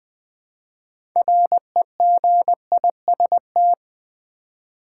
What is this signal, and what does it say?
Morse code, heard as a keyed tone, about 20 wpm, 700 Hz. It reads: REGIST
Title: Build A Word - 2 Letters At A Time - 8-Letter Words - 20wpm